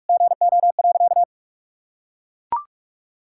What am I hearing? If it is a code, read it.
Morse code, heard as a keyed tone, about 41 wpm, 700 Hz. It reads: GO.
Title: Sentences from Top 1000 Words - Review Part 6 - 45wpm